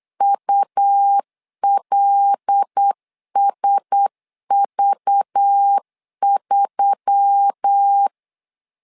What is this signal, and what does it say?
Morse code, heard as a keyed tone, about 8 wpm, 800 Hz. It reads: ULSV3
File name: LEZIONI / 5 LBVC / 5 LBV 8wpm 2837